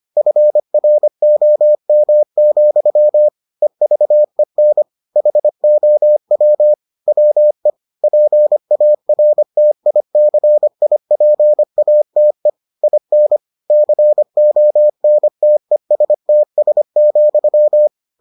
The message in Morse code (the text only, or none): FROM, EVEN HOW WE PARTICIPATE IN CONTESTS,